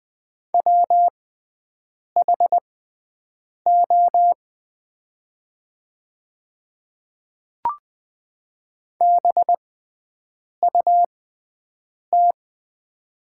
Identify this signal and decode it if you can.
Morse code, heard as a keyed tone, about 20 wpm, spaced as for 7 wpm, 700 Hz. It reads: WHO BUT
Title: 6x Character Spacing - Top 100 Words - 20wpm